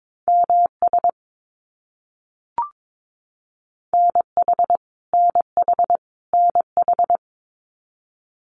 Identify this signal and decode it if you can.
Morse code, heard as a keyed tone, about 22 wpm, 700 Hz. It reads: MS NH NH NH
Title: US States Repeated 3x 22wpm